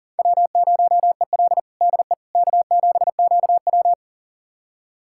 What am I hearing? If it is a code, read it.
Morse code, heard as a keyed tone, about 40 wpm, 700 Hz. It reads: W0EL DE K7QW
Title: USA Callsign DE Callsign - 4 Characters - 40wpm